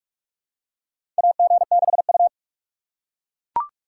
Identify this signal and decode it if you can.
Morse code, heard as a keyed tone, about 41 wpm, 700 Hz. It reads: AG6U
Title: USA Call Signs - 4 Characters - 6x Character Spacing - 45wpm